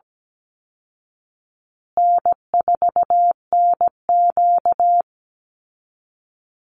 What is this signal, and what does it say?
Morse code, heard as a keyed tone, about 17 wpm, 700 Hz. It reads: N4NQ
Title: Call Signs Repeated 2x - 17wpm